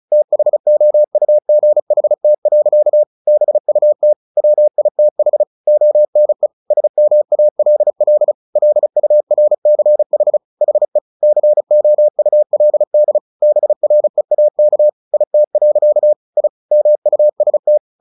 Morse code, heard as a keyed tone, about 35 wpm, 600 Hz. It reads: THOUGHT. BUT WITH ONE SMALL LURCH HE COULD BREAK IT. I MUST
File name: the old man and the sea 2 35wpm